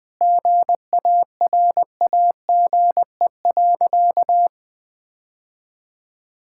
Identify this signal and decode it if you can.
Morse code, heard as a keyed tone, about 20 wpm, 700 Hz. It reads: GARAGE.
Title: Sentences from Top 1600 Words 20wpm